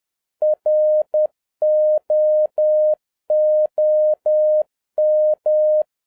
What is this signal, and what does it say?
Morse code, heard as a keyed tone, about 10 wpm, 610 Hz. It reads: ROOM
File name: 2011-07-mp3 / n56 BACK FROM A BIG ACCIDENT 10wpm